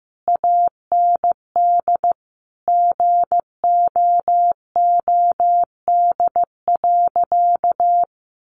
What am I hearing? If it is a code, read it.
Morse code, heard as a keyed tone, about 15 wpm, 700 Hz. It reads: AND GOOD.
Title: Sentences from Top 1500 Words 15wpm